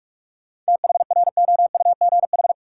Morse code, heard as a keyed tone, about 42 wpm, 700 Hz. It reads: THROUGH
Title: Sentences from Top 200 Words - New Word Drill - 45wpm